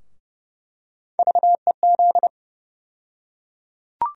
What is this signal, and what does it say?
Morse code, heard as a keyed tone, about 31 wpm, 700 Hz. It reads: VEZ
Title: USA Call Sign Postfixes - 3 Letters 30wpm